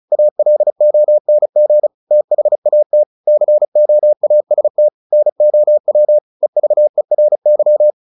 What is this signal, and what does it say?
Morse code, heard as a keyed tone, about 34 wpm, 600 Hz. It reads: ALONG THAT COAST NOW EVERY
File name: the old man and the sea / the old man and the sea 1 35wpm